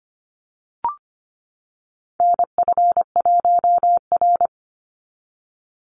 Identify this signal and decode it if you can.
Morse code, heard as a keyed tone, about 25 wpm, 700 Hz. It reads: NF1R
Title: NA SOTA Chaser Callsigns 25wpm